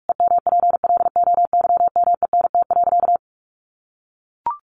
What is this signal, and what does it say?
Morse code, heard as a keyed tone, about 44 wpm, 700 Hz. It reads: EMPLOYMENT.
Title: Sentences from Top 1500 Words 45wpm